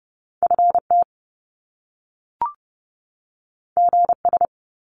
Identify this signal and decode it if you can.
Morse code, heard as a keyed tone, about 30 wpm, 700 Hz. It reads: FT GS
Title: Top Two Letter Combinations in Words 30wpm